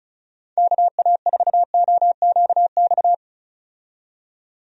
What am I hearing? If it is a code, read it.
Morse code, heard as a keyed tone, about 35 wpm, 700 Hz. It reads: KA4OQX
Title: USA Call Signs - 6 Characters - 35wpm